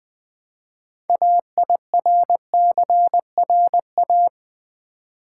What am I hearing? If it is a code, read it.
Morse code, heard as a keyed tone, about 20 wpm, 700 Hz. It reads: AIRCRA
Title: Build A Word - 3 Letters At A Time - 8-Letter Words - 20wpm